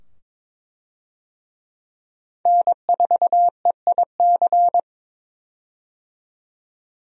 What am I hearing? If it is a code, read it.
Morse code, heard as a keyed tone, about 22 wpm, 700 Hz. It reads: N4EIC
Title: USA Call Signs - 5 Characters - 22wpm